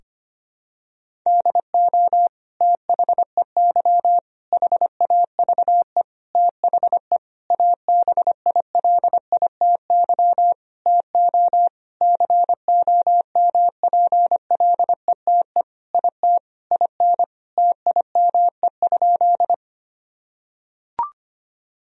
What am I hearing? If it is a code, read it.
Morse code, heard as a keyed tone, about 25 wpm, 700 Hz. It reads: DO THEY HAVE THE ABILITY TO COMPLETE IT IN TIME?
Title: Sentences from Top 1000 Words - Review Part 3 - 25wpm